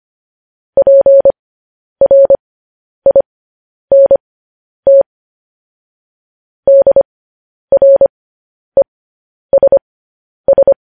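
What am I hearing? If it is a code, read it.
Morse code, heard as a keyed tone, about 25 wpm, spaced as for 10 wpm, 550 Hz. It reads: PRINT DRESS
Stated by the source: Morse code news headlines practice, at 10 WPM.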